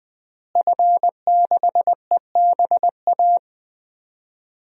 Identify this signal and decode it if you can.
Morse code, heard as a keyed tone, about 20 wpm, 700 Hz. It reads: F6EBA
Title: DXCC-100 Call Signs - 5 Characters - Repeated 3x - 20wpm